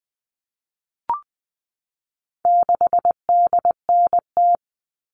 Morse code, heard as a keyed tone, about 20 wpm, 700 Hz. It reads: 6DNT
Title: All Cut Numbers - 4-Digit 20wpm